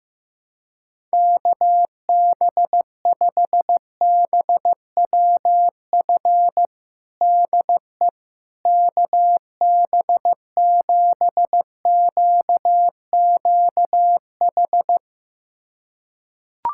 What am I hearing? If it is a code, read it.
Morse code, heard as a keyed tone, about 15 wpm, 700 Hz. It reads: KB5BWF DE KB7QQH
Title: USA Callsign DE Callsign 15wpm